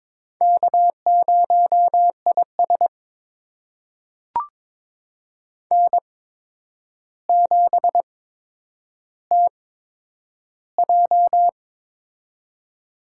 Morse code, heard as a keyed tone, about 22 wpm, spaced as for 9 wpm, 700 Hz. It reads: K0IS N7TJ
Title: USA Call Signs - 4 Characters - 8x Character Spacing - 22wpm